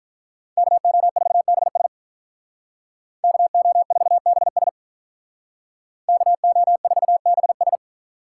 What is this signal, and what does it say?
Morse code, heard as a keyed tone, about 43 wpm, 700 Hz. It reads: KO4BS KO4BS KO4BS
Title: Speed Racing - USA 2x2 Call Signs 35wpm